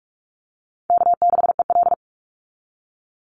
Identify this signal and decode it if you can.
Morse code, heard as a keyed tone, about 44 wpm, 690 Hz. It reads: K6EL
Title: NA SOTA Chaser Callsigns 45wpm